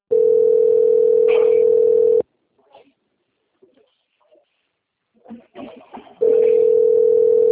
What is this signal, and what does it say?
Ringback tone (North America), 440 + 480 Hz.
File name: out- - - - - .42754